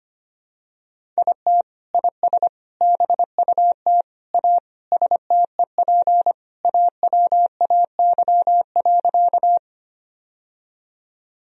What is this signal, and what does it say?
Morse code, heard as a keyed tone, about 25 wpm, 700 Hz. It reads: IT IS BUT A STEP AWAY.
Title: Sentences from Top 1000 Words - Review Part 2 - 25wpm